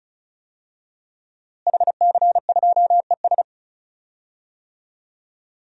Morse code, heard as a keyed tone, about 34 wpm, 700 Hz. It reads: HC2ES
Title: DXCC-100 Call Signs - 5 Characters - Repeated 3x - 35wpm